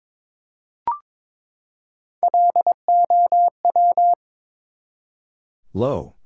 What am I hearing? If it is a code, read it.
Morse code, heard as a keyed tone, about 22 wpm, 700 Hz. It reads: LOW
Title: Three Letter Words - 22wpm